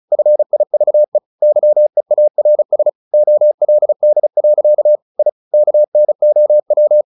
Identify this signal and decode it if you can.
Morse code, heard as a keyed tone, about 34 wpm, 600 Hz. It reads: FIVE YEARS OLD. I KNOW
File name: the old man and the sea / the old man and the sea 1 35wpm